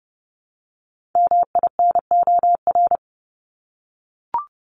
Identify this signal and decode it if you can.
Morse code, heard as a keyed tone, about 30 wpm, 700 Hz. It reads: MINOR
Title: Top 1251-1500 Words 30wpm